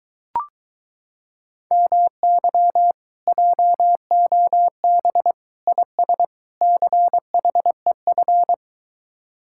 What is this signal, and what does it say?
Morse code, heard as a keyed tone, about 22 wpm, 700 Hz. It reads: MY JOB IS CHEF